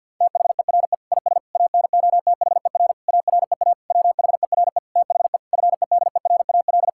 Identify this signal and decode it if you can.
Morse code, heard as a keyed tone, about 49 wpm, 700 Hz. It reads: THERE IS ANOTHER AREA WHERE THE FEDERAL
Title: Sentences from Top 1000 Words - Review Part 5 - 50wpm